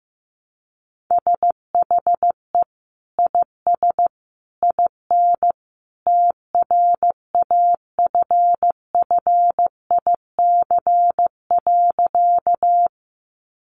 Morse code, heard as a keyed tone, about 15 wpm, 700 Hz. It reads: SHE IS IN TRAFFIC.
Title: Sentences from Top 1300 Words 15wpm